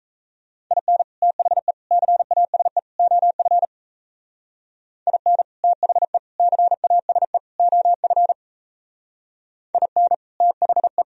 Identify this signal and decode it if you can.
Morse code, heard as a keyed tone, about 33 wpm, 700 Hz. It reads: IN THE CASE OF IN THE CASE OF IN THE
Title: Speed-Racing - Sets of 4 Words 28wpm